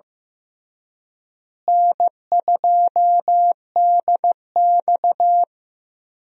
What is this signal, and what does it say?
Morse code, heard as a keyed tone, about 15 wpm, 700 Hz. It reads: N2DX
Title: USA Call Signs - 4 Characters - Repeated 3x - 15wpm